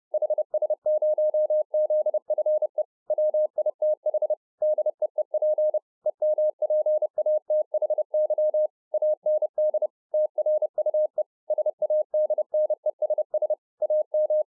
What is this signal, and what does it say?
Morse code, heard as a keyed tone, about 30 wpm, 610 Hz. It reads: HS0ZFE WITH DEEP EMPATHY AND TRUE SADNESS AM